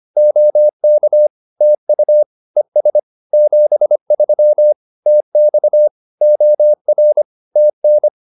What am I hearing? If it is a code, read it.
Morse code, heard as a keyed tone, about 25 wpm, 600 Hz. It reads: OK TU ES 73 TX OR TN